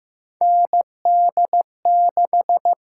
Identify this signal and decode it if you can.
Morse code, heard as a keyed tone, about 15 wpm, 700 Hz. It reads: ND6
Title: USA Call Signs - 4 Characters - 6x Character Spacing - 15wpm